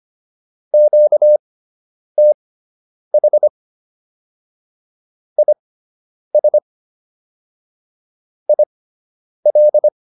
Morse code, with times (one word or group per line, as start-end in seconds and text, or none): 0.7-3.5 QTH
5.4-6.6 IS
8.5-9.9 IL